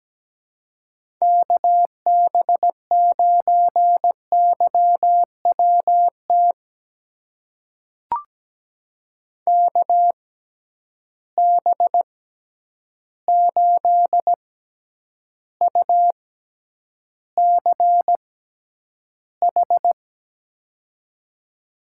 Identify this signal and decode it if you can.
Morse code, heard as a keyed tone, about 17 wpm, spaced as for 9 wpm, 700 Hz. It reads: KB9YWT KB8UCH
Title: USA Call Signs - 6 Characters - 6x Character Spacing - 17wpm